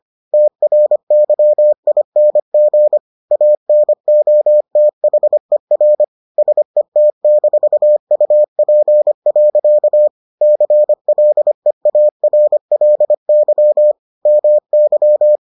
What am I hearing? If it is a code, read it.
Morse code, heard as a keyed tone, about 25 wpm, 600 Hz. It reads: TRYING ANOTHER SET-UP. CLEARLY MY